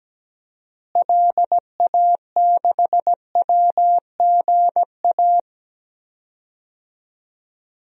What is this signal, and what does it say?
Morse code, heard as a keyed tone, about 17 wpm, 700 Hz. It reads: LA6WGA
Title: DXCC-100 Call Signs - 6 Characters - Repeated 2x - 17wpm